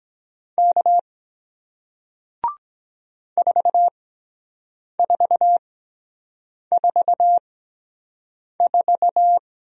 Morse code, heard as a keyed tone, about 23 wpm, spaced as for 7 wpm, 700 Hz. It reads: K 4444